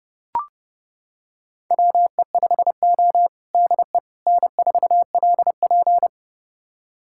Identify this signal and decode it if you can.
Morse code, heard as a keyed tone, about 32 wpm, 700 Hz. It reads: WE5O DE N4LP
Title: USA Callsign DE Callsign - 4 Characters - 30wpm